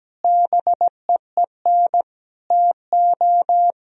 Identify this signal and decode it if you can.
Morse code, heard as a keyed tone, about 17 wpm, 700 Hz. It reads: BEEN TO
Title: Sentences from Top 1000 Words - Review Part 1 - 17wpm